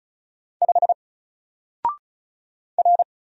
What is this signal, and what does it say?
Morse code, heard as a keyed tone, about 34 wpm, 700 Hz. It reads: 5 R
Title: Single Letter-Number 35wpm